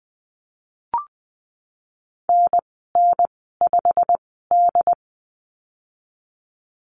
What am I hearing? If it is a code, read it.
Morse code, heard as a keyed tone, about 20 wpm, spaced as for 15 wpm, 700 Hz. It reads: NN5D
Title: 147 Lesson 24g - Letter y - Call Signs 20wpm